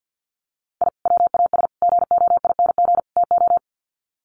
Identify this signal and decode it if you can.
Morse code, heard as a keyed tone, about 50 wpm, 700 Hz. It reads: I WAS GOING TO
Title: Sets of 4 Words 50wpm